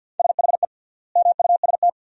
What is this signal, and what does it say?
Morse code, heard as a keyed tone, about 53 wpm, 700 Hz. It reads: SHE MUST
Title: 3x Word-Spacing - Sentences from Top 300 Words - 50wpm